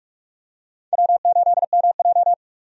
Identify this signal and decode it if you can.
Morse code, heard as a keyed tone, about 46 wpm, 700 Hz. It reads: W8MJ
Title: USA Call Signs - 4 Characters - 45wpm